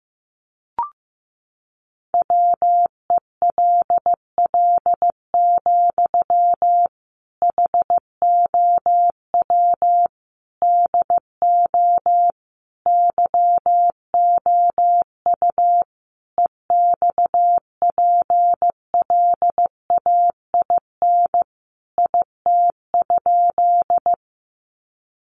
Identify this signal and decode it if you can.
Morse code, heard as a keyed tone, about 15 wpm, 700 Hz. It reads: WELL, HOW DO YOU EXPLAIN IT?
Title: Sentences from Top 1100 Words 15wpm